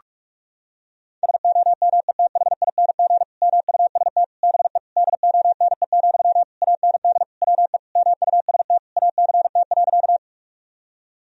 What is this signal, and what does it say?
Morse code, heard as a keyed tone, about 46 wpm, 700 Hz. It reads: SOMETHING MUST BE DONE, AND WE MUST ACT.
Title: Sentences from Top 1000 Words - Review Part 1 - 45wpm